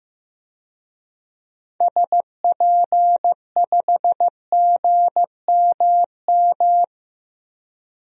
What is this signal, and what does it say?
Morse code, heard as a keyed tone, about 15 wpm, 700 Hz. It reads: SP5GMM
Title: DXCC-100 Call Signs - 6 Characters - Repeated 2x - 15wpm